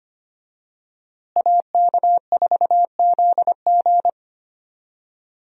Morse code, heard as a keyed tone, about 25 wpm, 700 Hz. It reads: AK4ZG
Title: USA Call Signs - 5 Characters - Repeated 2x - 25wpm